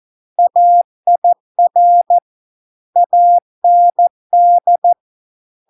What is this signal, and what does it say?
Morse code, heard as a keyed tone, about 14 wpm, 700 Hz. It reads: AIR AND